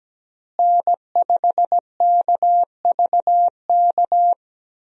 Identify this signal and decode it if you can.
Morse code, heard as a keyed tone, about 17 wpm, 700 Hz. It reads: N5KVK
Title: USA Call Signs - 5 Characters - Repeated 3x - 17wpm